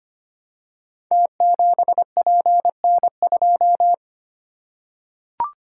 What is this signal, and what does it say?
Morse code, heard as a keyed tone, about 25 wpm, 700 Hz. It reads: T7PN2
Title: Code Groups - 5 Letter:Numbers - 25wpm